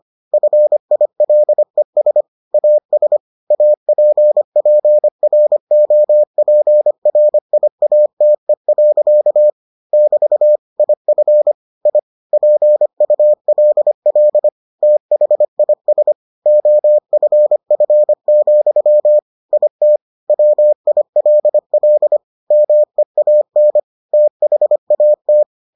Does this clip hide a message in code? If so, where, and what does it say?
Yes: Morse code from 0.3 to 25.4 s, heard as a keyed tone, about 25 wpm, 600 Hz. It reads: FILES AS APPROPRIATE. <BT> IF I PULL THIS OFF, IT WILL MEAN THAT